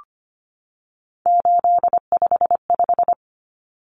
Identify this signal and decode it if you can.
Morse code, heard as a keyed tone, about 25 wpm, 700 Hz. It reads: 855
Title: Common Cut Numbers - 3-Digit 25wpm